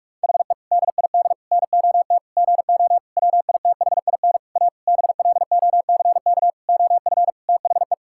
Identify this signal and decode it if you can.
Morse code, heard as a keyed tone, about 43 wpm, 700 Hz. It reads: HE DID NOT GO WITHIN A BLOCK OF THE
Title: Sentences from Top 1600 Words 45wpm